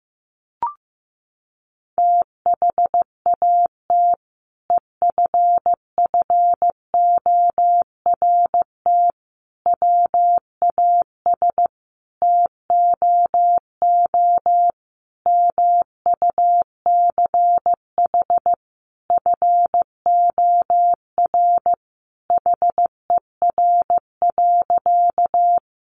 Morse code, heard as a keyed tone, about 15 wpm, 700 Hz. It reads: THAT EFFORT WAS TOO MUCH FOR HER.
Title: Sentences from Top 1200 Words 15wpm